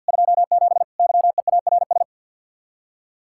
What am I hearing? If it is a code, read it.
Morse code, heard as a keyed tone, about 64 wpm, 700 Hz. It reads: 27 YEARS